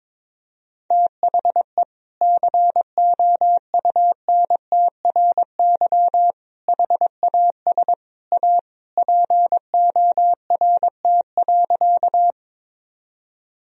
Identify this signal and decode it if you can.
Morse code, heard as a keyed tone, about 22 wpm, 700 Hz. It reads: THE COUNTRY HAS A PORT.
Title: Sentences from Top 1000 Words - Review Part 1 - 22wpm